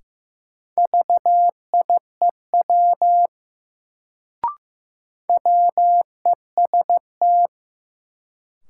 Morse code, heard as a keyed tone, about 15 wpm, 700 Hz. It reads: VIEW WEST
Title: Sets of 5 Words - New Word Drill - 15wpm